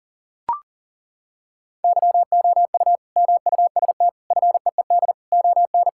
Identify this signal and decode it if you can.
Morse code, heard as a keyed tone, about 40 wpm, 700 Hz. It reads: YOU MUST FEED ON
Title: Sentences from Top 1600 Words 40wpm